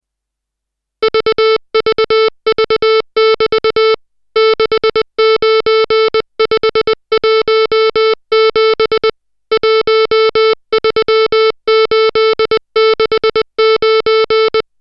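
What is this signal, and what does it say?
Morse code, heard as a keyed tone, about 20 wpm, 440 Hz, with noise. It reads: VVV<BT> 69517 13869
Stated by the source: Morse code drill of digits at 60 characters per minute.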